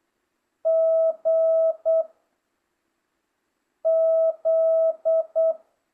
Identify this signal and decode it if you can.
Morse code, heard as a keyed tone, about 8 wpm, 660 Hz. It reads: G Z